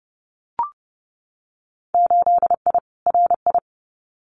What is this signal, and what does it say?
Morse code, heard as a keyed tone, about 30 wpm, 700 Hz. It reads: 8I RI
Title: Winter Field Day Exchanges Encore 30wpm